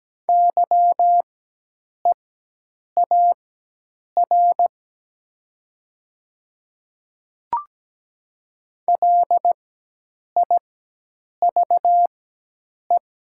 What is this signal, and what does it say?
Morse code, heard as a keyed tone, about 17 wpm, spaced as for 8 wpm, 700 Hz. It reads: YEAR LIVE
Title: 4x Character Spacing - Top 200 Words - 17wpm